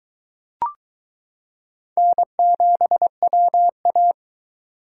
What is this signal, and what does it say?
Morse code, heard as a keyed tone, about 22 wpm, 700 Hz. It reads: N7WA